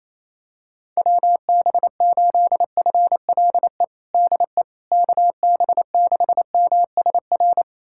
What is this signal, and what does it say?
Morse code, heard as a keyed tone, about 28 wpm, 700 Hz. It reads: WB8FLE DE KB6MSR
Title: USA Callsign DE Callsign - 6 Characters - 28wpm